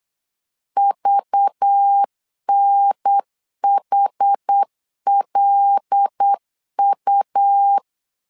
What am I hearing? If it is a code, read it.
Morse code, heard as a keyed tone, about 8 wpm, 800 Hz. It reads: VNHLU
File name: LEZIONI / 5 LBVC / 5 LBV 8wpm 2837